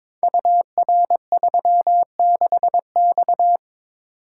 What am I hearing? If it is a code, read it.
Morse code, heard as a keyed tone, about 22 wpm, 700 Hz. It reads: UR36X
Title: Code Groups - 5 Letter:Numbers - 22wpm